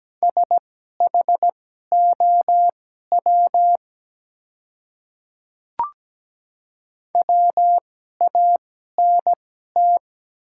Morse code, heard as a keyed tone, about 17 wpm, spaced as for 12 wpm, 700 Hz. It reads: SHOW WANT
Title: 2x Character Spacing - Top 200 Words - 17wpm